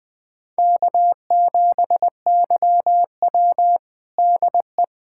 Morse code, heard as a keyed tone, about 20 wpm, 700 Hz. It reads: K7YW DE
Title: USA Callsign DE Callsign - 4 Characters - 20wpm